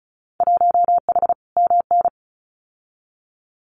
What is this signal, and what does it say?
Morse code, heard as a keyed tone, about 34 wpm, 700 Hz. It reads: 1H MN
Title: Winter Field Day Exchanges 35wpm